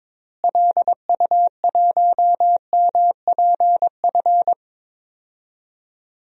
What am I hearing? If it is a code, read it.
Morse code, heard as a keyed tone, about 22 wpm, 700 Hz. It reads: LU1MPF
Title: DXCC-100 Call Signs - 6 Characters - Repeated 2x - 22wpm